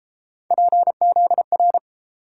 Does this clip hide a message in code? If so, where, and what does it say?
Yes: Morse code from 0.5 to 1.8 s, heard as a keyed tone, about 30 wpm, 700 Hz. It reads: PZR